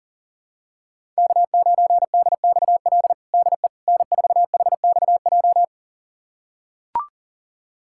Morse code, heard as a keyed tone, about 40 wpm, 700 Hz. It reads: K9DXL DE N4HXJ
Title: USA Callsign DE Callsign - 5 Characters - 40wpm